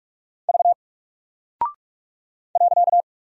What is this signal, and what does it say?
Morse code, heard as a keyed tone, about 44 wpm, 700 Hz. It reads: V .
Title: Single Character (Letter:Number:Punctuation:Prosign) Alt 45wpm